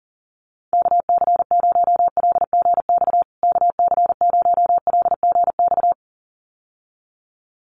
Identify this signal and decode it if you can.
Morse code, heard as a keyed tone, about 40 wpm, 700 Hz. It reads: KC0LGX KC0LGX
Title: USA Call Signs Repeated 2x 40wpm